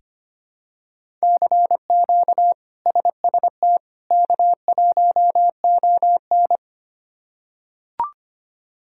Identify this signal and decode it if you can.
Morse code, heard as a keyed tone, about 25 wpm, 700 Hz. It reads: CQ SST K1ON
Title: K1USN SST QSO Practice - 25wpm